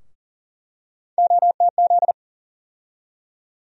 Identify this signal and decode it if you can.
Morse code, heard as a keyed tone, about 40 wpm, 700 Hz. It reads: OTZ
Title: USA Call Sign Postfixes - 3 Letters 40wpm